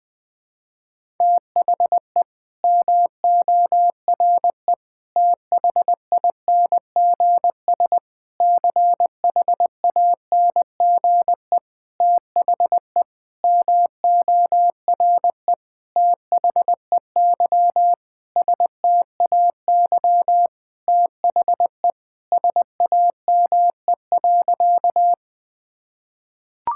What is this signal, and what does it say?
Morse code, heard as a keyed tone, about 21 wpm, 700 Hz. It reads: THE MORE THINGS CHANGE THE MORE THEY STAY THE SAME.
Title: Sentences with 8-Word Phrases - 20wpm